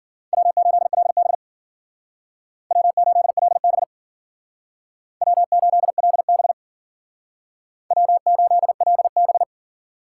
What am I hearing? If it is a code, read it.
Morse code, heard as a keyed tone, about 51 wpm, 700 Hz. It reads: W8LB W8LB W8LB W8LB